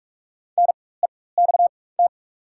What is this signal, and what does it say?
Morse code, heard as a keyed tone, about 43 wpm, spaced as for 21 wpm, 690 Hz. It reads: NEXT
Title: 4x Character Spacing - Top 500 Words - 45wpm